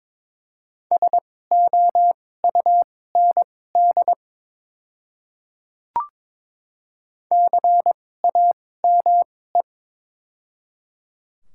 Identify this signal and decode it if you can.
Morse code, heard as a keyed tone, about 22 wpm, spaced as for 16 wpm, 700 Hz. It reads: SOUND CAME
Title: 2x Character Spacing - Top 300 Words - 22wpm